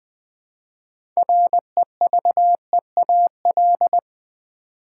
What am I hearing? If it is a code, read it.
Morse code, heard as a keyed tone, about 20 wpm, 700 Hz. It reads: REVEAL
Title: Build A Word - 2 Letters At A Time - 6-Letter Words - 20wpm